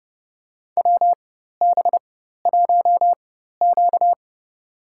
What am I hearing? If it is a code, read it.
Morse code, heard as a keyed tone, about 30 wpm, spaced as for 14 wpm, 700 Hz. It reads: WB1Q
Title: USA Call Signs - 4 Characters - 4x Character Spacing - 30wpm